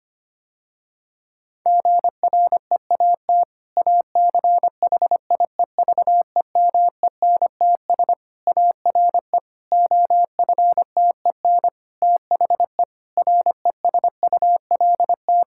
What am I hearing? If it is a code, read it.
Morse code, heard as a keyed tone, about 25 wpm, 700 Hz. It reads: GREAT ACHIEVEMENTS ARE OFTEN THE RESULT
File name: practice / 25wpm / Difficult Sentences with Hints - 25wpm